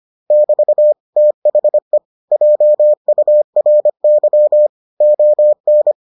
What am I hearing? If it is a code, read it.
Morse code, heard as a keyed tone, about 25 wpm, 600 Hz. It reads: <BT> THE JURY ON